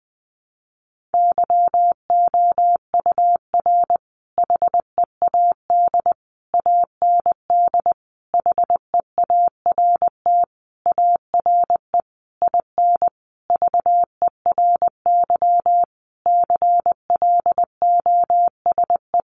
Morse code, heard as a keyed tone, about 20 wpm, 700 Hz. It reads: YOUR HEAD AND HEART ARE IN VERY CLOSE
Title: Sentences from Top 1300 Words 20wpm